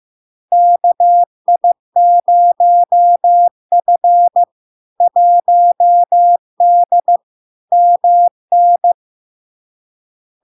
Morse code, heard as a keyed tone, about 15 wpm, 700 Hz. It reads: KI0F 1D MN
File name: LARK-FD-2022-Log-in-CW-15wpm-700-Hz